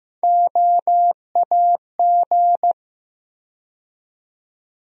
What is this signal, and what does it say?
Morse code, heard as a keyed tone, about 16 wpm, 700 Hz. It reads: OAG
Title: USA Call Sign Postfixes - 3 Letters 15wpm